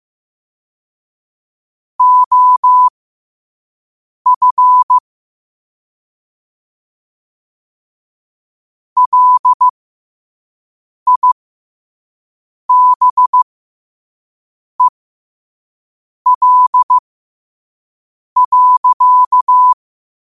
Morse code, heard as a keyed tone, about 15 wpm, spaced as for 5 wpm, 1000 Hz. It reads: OF LIBEL.
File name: qotdfiles / 5wpm